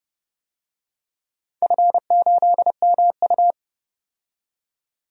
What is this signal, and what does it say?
Morse code, heard as a keyed tone, about 30 wpm, 700 Hz. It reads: F8MU
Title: DXCC-100 Call Signs - 4 Characters - 30wpm